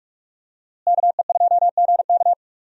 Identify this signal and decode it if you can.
Morse code, heard as a keyed tone, about 44 wpm, 700 Hz. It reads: KE2GK
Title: USA Call Signs - 5 Characters - Repeated 3x - 45wpm